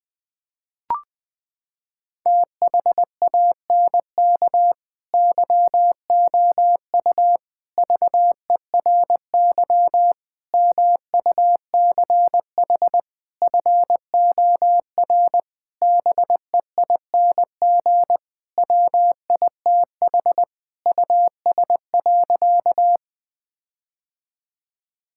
Morse code, heard as a keyed tone, about 20 wpm, 700 Hz. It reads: THANK YOU VERY MUCH FOR BEING WITH US.
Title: Sentences with 5-Word Phrases - 20wpm